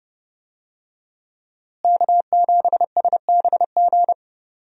Morse code, heard as a keyed tone, about 30 wpm, 700 Hz. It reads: K7SBG
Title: USA Call Signs - 5 Characters - Repeated 2x - 30wpm